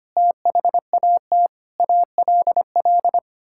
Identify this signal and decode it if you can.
Morse code, heard as a keyed tone, about 25 wpm, 700 Hz. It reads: THAT ALL
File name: Sets of 9 Words - 25wpm